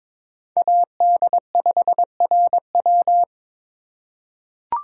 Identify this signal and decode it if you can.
Morse code, heard as a keyed tone, about 21 wpm, 700 Hz. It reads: AD5RW
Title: USA Call Signs - 5 Characters - 3x Character Spacing - 22wpm